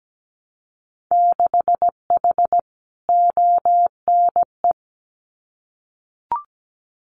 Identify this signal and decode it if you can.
Morse code, heard as a keyed tone, about 17 wpm, 700 Hz. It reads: 6H ONE
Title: Winter Field Day Exchanges Encore 17wpm